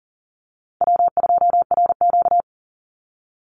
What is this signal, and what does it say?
Morse code, heard as a keyed tone, about 40 wpm, 700 Hz. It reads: W2RQ
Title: Easy Call Signs Repeated 3x - 40wpm